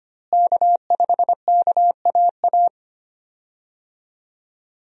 Morse code, heard as a keyed tone, about 25 wpm, 700 Hz. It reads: K5KAA
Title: USA Call Signs - 5 Characters - Repeated 2x - 25wpm